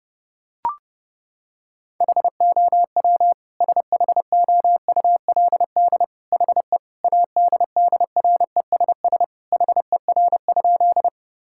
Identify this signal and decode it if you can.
Morse code, heard as a keyed tone, about 32 wpm, 700 Hz. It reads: HOW SHOULD HE ADDRESS HER?
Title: Sentences from Top 1000 Words - Review Part 6 - 30wpm